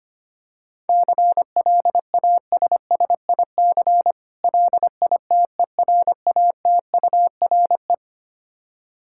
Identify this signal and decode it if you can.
Morse code, heard as a keyed tone, about 25 wpm, 700 Hz. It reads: CLASSIC LITERATURE
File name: practice / 25wpm / Difficult Words with Hints - 25wpm